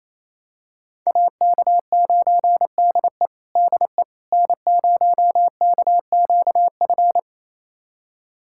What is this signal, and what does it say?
Morse code, heard as a keyed tone, about 28 wpm, 700 Hz. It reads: AK9DE DE N0KQF
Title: USA Callsign DE Callsign - 5 Characters - 28wpm